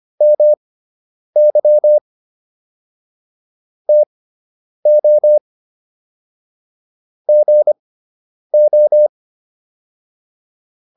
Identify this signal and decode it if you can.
Morse code, heard as a keyed tone, about 25 wpm, spaced as for 9 wpm, 600 Hz. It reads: MY TO GO